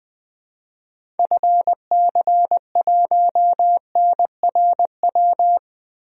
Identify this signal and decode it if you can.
Morse code, heard as a keyed tone, about 20 wpm, 700 Hz. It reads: FC1NRW
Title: DXCC-100 Call Signs - 6 Characters - Repeated 3x - 20wpm